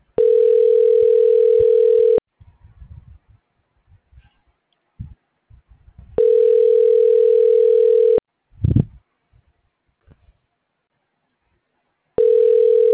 Ringback tone (North America), 440 + 480 Hz.
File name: recordings / RE6941a9b483e2ffa142f10f dd5